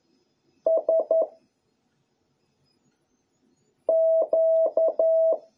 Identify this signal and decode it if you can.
Morse code, heard as a keyed tone, about 11 wpm, 650 Hz. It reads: S Q